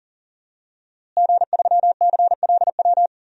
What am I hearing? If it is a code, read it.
Morse code, heard as a keyed tone, about 40 wpm, 700 Hz. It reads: G3CLW
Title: DXCC-100 Call Signs - 5 Characters - Repeated 3x - 40wpm